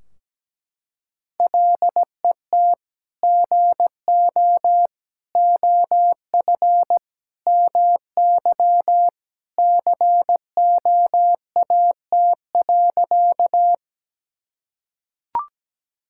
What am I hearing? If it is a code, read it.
Morse code, heard as a keyed tone, about 17 wpm, 700 Hz. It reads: LET GO OF MY COAT.
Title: Sentences from Top 1600 Words 17wpm